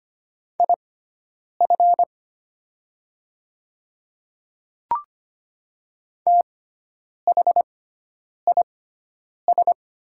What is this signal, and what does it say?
Morse code, heard as a keyed tone, about 25 wpm, spaced as for 9 wpm, 700 Hz. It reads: IF THIS
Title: 6x Character Spacing - Top 200 Words - 25wpm